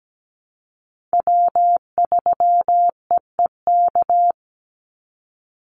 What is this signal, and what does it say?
Morse code, heard as a keyed tone, about 17 wpm, 700 Hz. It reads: W3EEK
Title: Call Signs - 17wpm